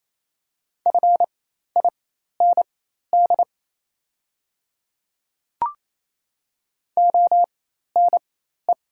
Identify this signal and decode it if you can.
Morse code, heard as a keyed tone, about 28 wpm, spaced as for 13 wpm, 700 Hz. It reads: FIND ONE
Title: 4x Character Spacing - Top 100 Words - 28wpm